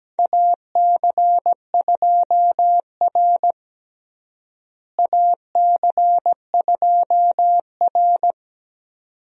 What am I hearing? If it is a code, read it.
Morse code, heard as a keyed tone, about 17 wpm, 700 Hz. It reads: AC2R AC2R
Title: USA Call Signs - 4 Characters - Repeated 2x - 17wpm